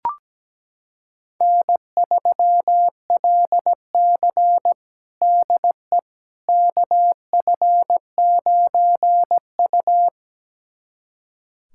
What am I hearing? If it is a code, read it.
Morse code, heard as a keyed tone, about 17 wpm, 700 Hz. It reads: N3LC DE KF9U
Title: USA Callsign DE Callsign - 4 Characters - 17wpm